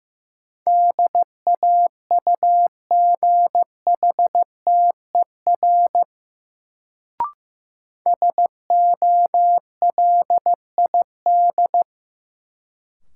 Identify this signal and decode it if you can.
Morse code, heard as a keyed tone, about 16 wpm, 700 Hz. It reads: DAUGHTER SOLID
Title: Sentences from Top 1500 Words - New Word Drill - 15wpm